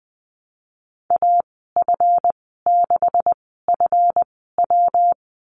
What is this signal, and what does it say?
Morse code, heard as a keyed tone, about 20 wpm, spaced as for 15 wpm, 700 Hz. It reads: AF6FW
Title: 186 Lesson 30g - Letter b - Call Signs 20wpm